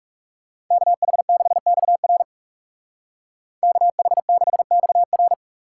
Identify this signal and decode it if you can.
Morse code, heard as a keyed tone, about 39 wpm, 700 Hz. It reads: KH6XR KH6XR